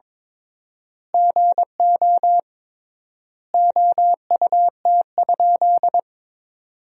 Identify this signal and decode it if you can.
Morse code, heard as a keyed tone, about 22 wpm, 700 Hz. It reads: GO OUT?
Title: 3x Word-Spacing - Sentences from Top 300 Words - 22wpm